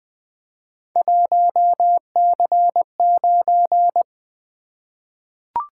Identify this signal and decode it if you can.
Morse code, heard as a keyed tone, about 20 wpm, 700 Hz. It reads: 1C9
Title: Three Letter-Number Combinations - 20wpm